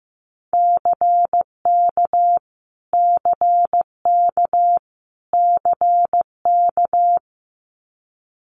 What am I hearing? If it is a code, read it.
Morse code, heard as a keyed tone, about 15 wpm, 700 Hz. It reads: CK CK CK
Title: Multi-letter Phonemes Repeated 3x 15wpm